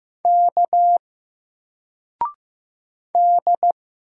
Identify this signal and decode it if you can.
Morse code, heard as a keyed tone, about 15 wpm, 700 Hz. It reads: K D
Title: Single Letter-Number 15wpm